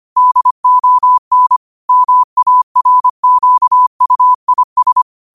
Morse code, heard as a keyed tone, about 25 wpm, 1000 Hz. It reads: DON MARQUIS
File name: qotdfiles / 25wpm